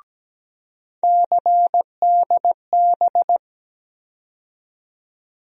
Morse code, heard as a keyed tone, about 17 wpm, 700 Hz. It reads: CDB
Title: USA Call Sign Postfixes - 3 Letters 17wpm